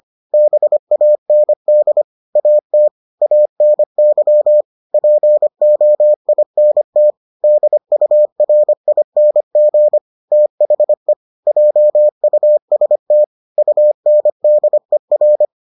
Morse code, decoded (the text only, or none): BAND AT ANY POINT DURING THE JUST UNDER